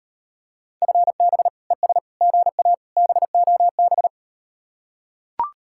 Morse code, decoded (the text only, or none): FB ES GA BOB